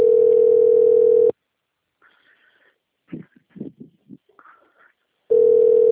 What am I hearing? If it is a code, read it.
Ringback tone (North America), 440 + 480 Hz.